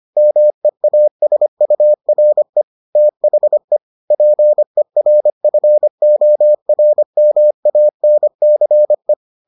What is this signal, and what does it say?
Morse code, heard as a keyed tone, about 25 wpm, 600 Hz. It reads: MEASURE THE PERFORMANCE